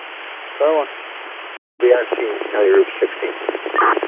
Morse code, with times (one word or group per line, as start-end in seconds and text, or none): none